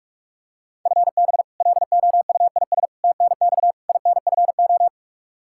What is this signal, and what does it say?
Morse code, heard as a keyed tone, about 44 wpm, 700 Hz. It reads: FB LOUIS TNX INFO
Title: Practice QSO - Exchange 3 - More Info - 45wpm